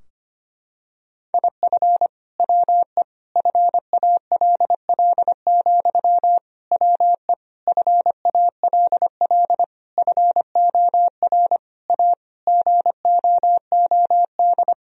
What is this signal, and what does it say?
Morse code, heard as a keyed tone, about 25 wpm, 700 Hz. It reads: IF WE FALL, WE FALL FOR A GOOD
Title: Sentences from Top 1000 Words - Review Part 2 - 25wpm